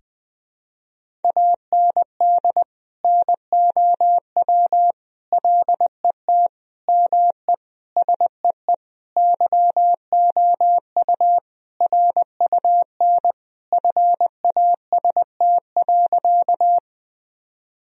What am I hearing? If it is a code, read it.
Morse code, heard as a keyed tone, about 20 wpm, 700 Hz. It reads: AND NOW LET ME SEE YOU RUN FAST.
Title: Sentences from Top 1000 Words - Review Part 2 - 20wpm